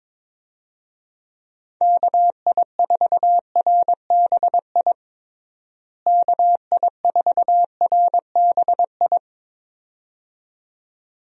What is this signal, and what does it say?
Morse code, heard as a keyed tone, about 22 wpm, 700 Hz. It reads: KI4RBI KI4RBI
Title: USA Call Signs - 6 Characters - Repeated 2x - 22wpm